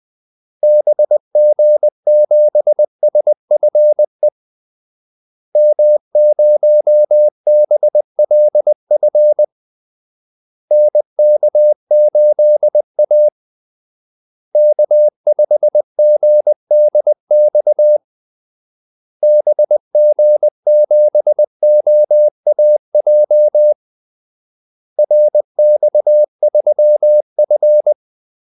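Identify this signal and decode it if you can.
Morse code, heard as a keyed tone, about 20 wpm, 600 Hz. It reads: BG7SFE M0BLF NK8A K5GDX BG7OAJ RX3F
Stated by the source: Morse call-sign practice at 20 wpm.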